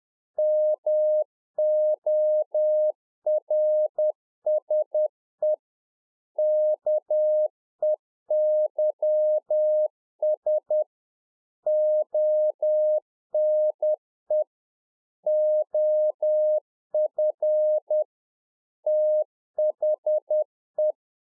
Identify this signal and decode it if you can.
Morse code, heard as a keyed tone, about 10 wpm, 610 Hz. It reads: MORSE KEYS ONE OF THE
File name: Czech Army Morse Keys 10WPM